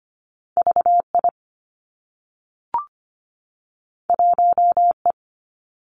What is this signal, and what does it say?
Morse code, heard as a keyed tone, about 25 wpm, 700 Hz. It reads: VI 1E
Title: Two Challenging Letter-Number Combinations 25wpm